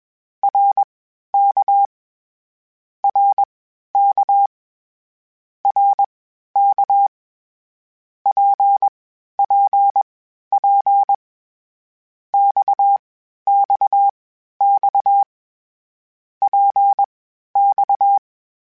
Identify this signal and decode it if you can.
Morse code, heard as a keyed tone, about 21 wpm, spaced as for 12 wpm, 800 Hz. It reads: RK RK RK PPP XXX PX
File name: Short Runs / Short Runs-mp3s / Balanced @20